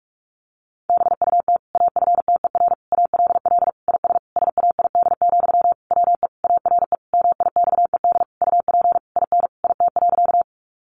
Morse code, heard as a keyed tone, about 44 wpm, 700 Hz. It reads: BUT AFTER ALL IS SAID, WE ARE MIXED UP IN IT.
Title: Sentences from Top 1500 Words 45wpm